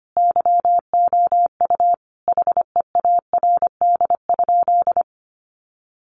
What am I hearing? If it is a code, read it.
Morse code, heard as a keyed tone, about 25 wpm, 700 Hz. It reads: YOU HEARD?
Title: Sentences from Top 400 Words 25wpm